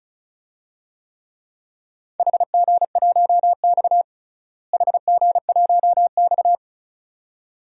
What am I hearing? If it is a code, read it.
Morse code, heard as a keyed tone, about 34 wpm, 700 Hz. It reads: HG1X HG1X
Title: DXCC-100 Call Signs - 4 Characters - Repeated 2x - 35wpm